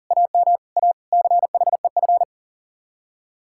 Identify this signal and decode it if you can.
Morse code, heard as a keyed tone, about 53 wpm, spaced as for 47 wpm, 700 Hz. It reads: AM A CHEF
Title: Speed-Racing Common QSO Phrases Encore - 40wpm